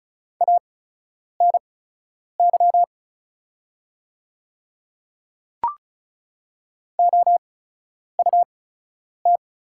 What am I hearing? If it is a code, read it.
Morse code, heard as a keyed tone, about 36 wpm, spaced as for 10 wpm, 700 Hz. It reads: ANY OUT
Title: 8x Character Spacing - Top 200 Words - 35wpm